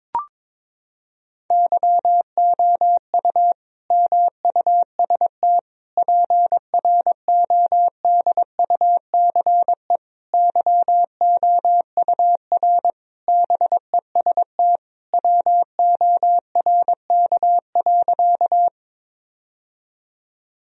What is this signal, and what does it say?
Morse code, heard as a keyed tone, about 22 wpm, 700 Hz. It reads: YOU MUST PRODUCE YOUR BEST WORK.
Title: Sentences from Top 1000 Words - Review Part 3 - 22wpm